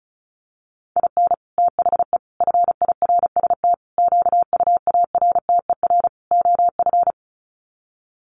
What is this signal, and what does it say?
Morse code, heard as a keyed tone, about 34 wpm, 700 Hz. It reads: IN THE FIRST QUARTER OF
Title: Sets of 5 Words Encore2 35wpm